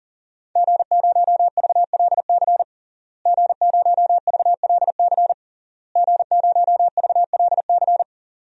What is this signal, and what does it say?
Morse code, heard as a keyed tone, about 37 wpm, 700 Hz. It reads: G0VLC G0VLC G0VLC
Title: DXCC-100 Call Signs - 5 Characters - Repeated 3x - 40wpm